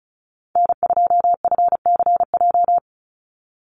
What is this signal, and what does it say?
Morse code, heard as a keyed tone, about 34 wpm, 700 Hz. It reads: N2FCJ
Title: USA Call Signs 35wpm